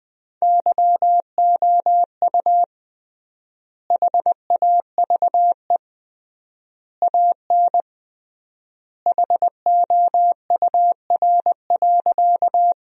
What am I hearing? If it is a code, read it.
Morse code, heard as a keyed tone, about 21 wpm, 700 Hz. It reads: YOU HAVE AN HOUR.
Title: 3x Word-Spacing - Sentences from Top 400 Words - 20wpm